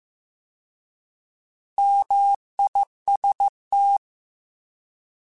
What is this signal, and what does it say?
Morse code, heard as a keyed tone, about 15 wpm, 800 Hz. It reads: MIST
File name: Alphabetical / Alphabetical mp3s / Ms @13